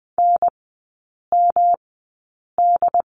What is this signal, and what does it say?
Morse code, heard as a keyed tone, about 20 wpm, spaced as for 8 wpm, 700 Hz. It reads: NMD
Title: 107 Lesson 18e - Letter m - Three Characters 20wpm